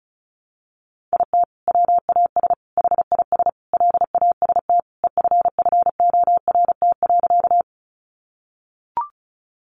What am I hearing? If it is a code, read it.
Morse code, heard as a keyed tone, about 34 wpm, 700 Hz. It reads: IT WAS HIS LAST EFFORT.
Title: Sentences from Top 1200 Words 35wpm